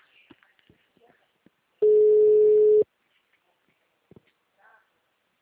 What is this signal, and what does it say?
Ringback tone (Europe, 425 Hz), with speech.